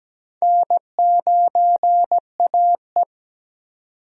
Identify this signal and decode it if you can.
Morse code, heard as a keyed tone, about 17 wpm, 700 Hz. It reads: N9AE
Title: USA Call Signs - 4 Characters - 17wpm